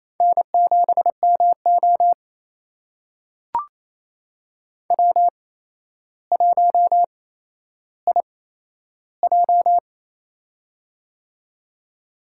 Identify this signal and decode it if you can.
Morse code, heard as a keyed tone, about 28 wpm, spaced as for 12 wpm, 700 Hz. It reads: N7MO W1IJ
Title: USA Call Signs - 4 Characters - 8x Character Spacing - 28wpm